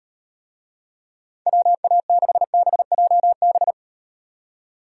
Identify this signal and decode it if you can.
Morse code, heard as a keyed tone, about 38 wpm, 700 Hz. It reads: WA6BJB